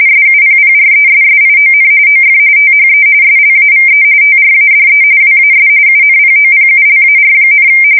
RTTY, about 45.45 baud. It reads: SUPPRESSION OF THE SLAVE TRADE WAS UNPOPULAR,
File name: fsk170 / usnavy2